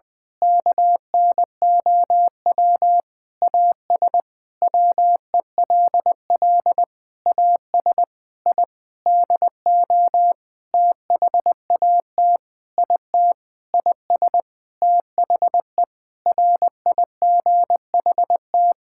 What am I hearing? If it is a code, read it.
Morse code, heard as a keyed tone, about 21 wpm, 700 Hz. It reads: KNOW AS WELL AS I DO THAT IT IS THE RIGHT
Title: Sentences with 8-Word Phrases - 20wpm